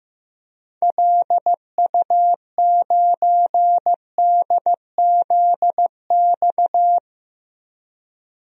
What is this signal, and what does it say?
Morse code, heard as a keyed tone, about 15 wpm, 700 Hz. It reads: LU9DZX
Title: DXCC-100 Call Signs - 6 Characters - 15wpm